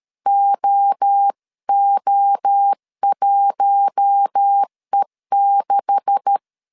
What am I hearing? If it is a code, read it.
Morse code, heard as a keyed tone, about 13 wpm, 800 Hz. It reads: OO1E6
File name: CW / DET / 1 9 1500C S B 12wpm 3819